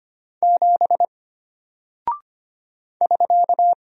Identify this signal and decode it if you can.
Morse code, heard as a keyed tone, about 25 wpm, 700 Hz. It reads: 7 <SK>
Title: Single Character (Letter:Number:Punctuation:Prosign) Alt 25wpm